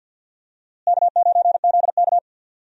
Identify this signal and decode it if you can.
Morse code, heard as a keyed tone, about 52 wpm, 700 Hz. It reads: K9ZK
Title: USA Call Signs - 4 Characters - Repeated 2x - 50wpm